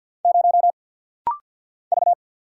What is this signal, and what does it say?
Morse code, heard as a keyed tone, about 50 wpm, 700 Hz. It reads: 0 V
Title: Single Letter-Number Alt - Rapid-Fire 50wpm